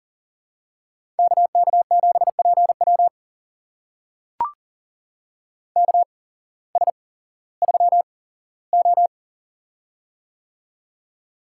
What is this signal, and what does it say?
Morse code, heard as a keyed tone, about 40 wpm, spaced as for 19 wpm, 700 Hz. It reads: KK7PW KS3O
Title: USA Call Signs - 8x Character Spacing - 40wpm